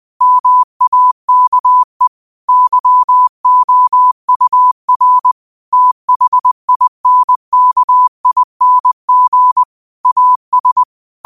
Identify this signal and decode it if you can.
Morse code, heard as a keyed tone, about 20 wpm, 1000 Hz. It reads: MAKE YOUR THINKING AS